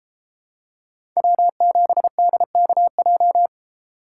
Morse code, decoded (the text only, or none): W7DKJ